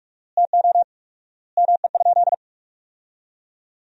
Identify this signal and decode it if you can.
Morse code, heard as a keyed tone, about 41 wpm, 700 Hz. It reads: TO ME?
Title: 4x Word-Spacing - Sentences from Top 200 Words - 45wpm